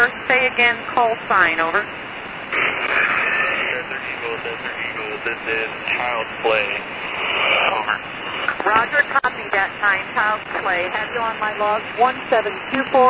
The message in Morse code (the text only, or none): none